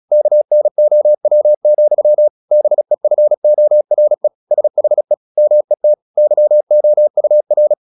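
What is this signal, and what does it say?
Morse code, heard as a keyed tone, about 36 wpm, 600 Hz. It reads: KNOW, BEFORE SHE MET YOUR
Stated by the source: Morse practice at 36 wpm, text of Ken Follett's novel Never, from CW med Gnister.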